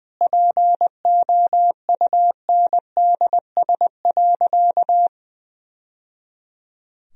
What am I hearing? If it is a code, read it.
Morse code, heard as a keyed tone, about 20 wpm, 700 Hz. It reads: POUNDS.
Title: Sentences from Top 1000 Words - Review Part 2 - 20wpm